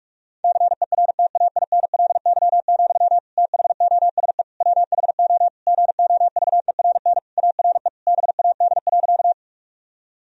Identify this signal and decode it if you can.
Morse code, heard as a keyed tone, about 46 wpm, 700 Hz. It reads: CERTAINLY, THOSE WHO GOVERN ARE BAD.
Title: Sentences from Top 1000 Words - Review Part 2 - 45wpm